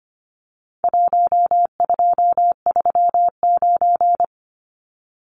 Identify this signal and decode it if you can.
Morse code, heard as a keyed tone, about 25 wpm, 700 Hz. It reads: 1239
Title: Numbers - 4-Digit 25wpm